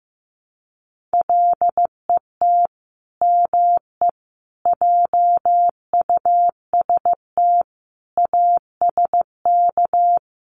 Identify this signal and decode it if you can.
Morse code, heard as a keyed tone, about 15 wpm, 700 Hz. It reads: LET ME JUST ASK
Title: Sets of 5 Words Encore2 15wpm